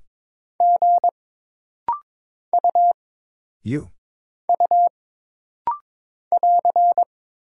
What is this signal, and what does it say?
Morse code, heard as a keyed tone, about 22 wpm, 700 Hz. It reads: G U U <AR>
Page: This rapid-fire timing is geared to push your ICR (Instant Character Recognition) skills.